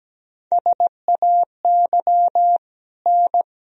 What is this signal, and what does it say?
Morse code, heard as a keyed tone, about 17 wpm, 700 Hz. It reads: SAY N
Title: Sentences from Top 200 Words and Abbreviations 17wpm